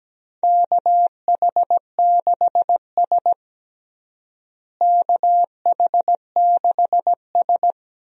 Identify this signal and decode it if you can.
Morse code, heard as a keyed tone, about 17 wpm, 700 Hz. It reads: KH6S KH6S
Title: DXCC-100 Call Signs - 4 Characters - Repeated 3x - 17wpm